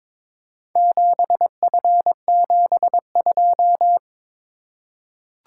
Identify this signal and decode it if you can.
Morse code, heard as a keyed tone, about 22 wpm, 700 Hz. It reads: 7F72
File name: practice / 22wpm / Four Letter-Number Combinations - 22wpm